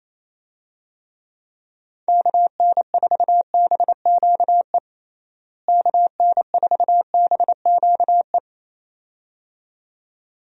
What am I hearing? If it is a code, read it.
Morse code, heard as a keyed tone, about 28 wpm, 700 Hz. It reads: KN4BQE KN4BQE
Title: USA Call Signs - 6 Characters - Repeated 2x - 28wpm